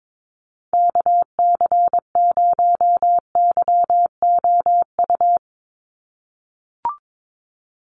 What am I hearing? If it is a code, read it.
Morse code, heard as a keyed tone, about 22 wpm, 700 Hz. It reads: KC0YOU
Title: USA Call Signs 22wpm